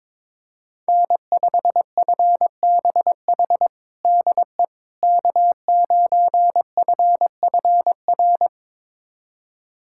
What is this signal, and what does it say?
Morse code, heard as a keyed tone, about 22 wpm, 700 Hz. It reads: N5FBH DE K9FFR
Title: USA Callsign DE Callsign - 5 Characters - 22wpm